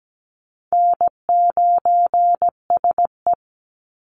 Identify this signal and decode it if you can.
Morse code, heard as a keyed tone, about 17 wpm, 700 Hz. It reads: N9SE
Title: Easy Call Signs Repeated 2x - 17wpm